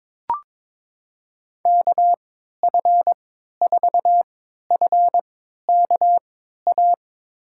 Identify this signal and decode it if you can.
Morse code, heard as a keyed tone, about 23 wpm, spaced as for 13 wpm, 700 Hz. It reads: KF4FKA
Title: USA Call Signs - 6 Characters - 3x Character Spacing - 22wpm